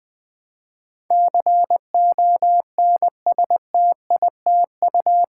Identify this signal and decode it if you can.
Morse code, heard as a keyed tone, about 20 wpm, 700 Hz. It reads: CONSTITU
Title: Build A Word - 4 Letters At A Time - 12-Letter Words - 20wpm